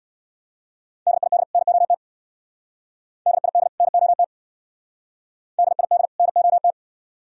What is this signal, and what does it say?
Morse code, heard as a keyed tone, about 59 wpm, 700 Hz. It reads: DID NOT DID NOT DID NOT
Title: Speed-Racing - Sets of 2 Words 50wpm